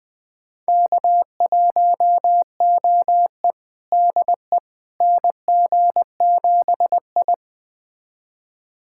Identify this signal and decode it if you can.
Morse code, heard as a keyed tone, about 20 wpm, 700 Hz. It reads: K1OE DE NG7I
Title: USA Callsign DE Callsign - 4 Characters - 20wpm